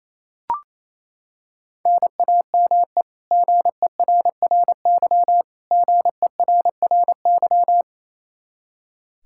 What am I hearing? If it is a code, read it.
Morse code, heard as a keyed tone, about 28 wpm, 700 Hz. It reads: NAME GERRY GERRY
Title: Practice QSO - Exchange 1 - Beginning - 28wpm